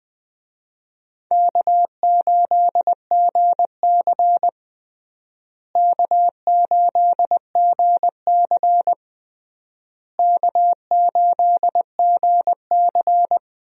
Morse code, heard as a keyed tone, about 20 wpm, 700 Hz. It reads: K8GC K8GC K8GC
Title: USA Call Signs - 4 Characters - Repeated 3x - 20wpm